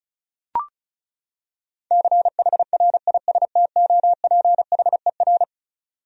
Morse code, heard as a keyed tone, about 36 wpm, 700 Hz. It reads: CHRISTOPHER
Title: USA Common Names - 35wpm